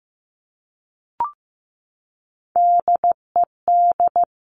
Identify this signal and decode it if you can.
Morse code, heard as a keyed tone, about 15 wpm, 700 Hz. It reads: DED
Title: Top Three Letter Combinations in Words 15wpm